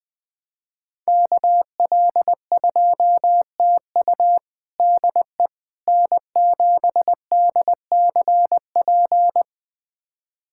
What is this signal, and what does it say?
Morse code, heard as a keyed tone, about 20 wpm, 700 Hz. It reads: KL2TU DE N7DCP
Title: USA Callsign DE Callsign - 5 Characters - 20wpm